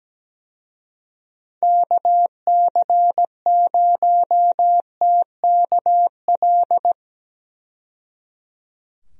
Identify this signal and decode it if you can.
Morse code, heard as a keyed tone, about 17 wpm, 700 Hz. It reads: KC0TKL
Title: USA Call Signs - 6 Characters - 17wpm